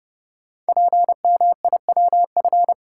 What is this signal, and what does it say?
Morse code, heard as a keyed tone, about 31 wpm, 700 Hz. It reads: PMIWF
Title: Code Groups - 5 Letter:Numbers - 30wpm